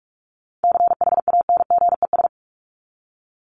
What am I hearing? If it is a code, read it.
Morse code, heard as a keyed tone, about 44 wpm, 700 Hz. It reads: CHANGES
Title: ES:S Suffix - Top 100 Words with Base Words 45wpm